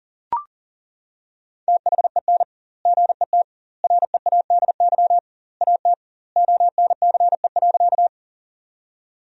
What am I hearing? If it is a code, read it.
Morse code, heard as a keyed tone, about 40 wpm, 700 Hz. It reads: THEN GET READY AT ONCE.
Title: 2x Word-Spacing - Sentences from Top 400 Words - 40wpm